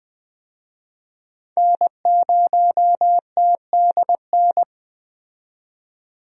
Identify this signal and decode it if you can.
Morse code, heard as a keyed tone, about 20 wpm, 700 Hz. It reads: N0TDN
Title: USA Call Signs - 5 Characters - 20wpm